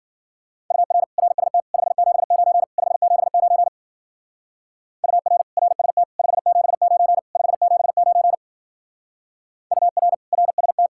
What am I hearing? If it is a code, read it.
Morse code, heard as a keyed tone, about 50 wpm, 700 Hz. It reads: UR RST 579 579 UR RST 579 579 UR RST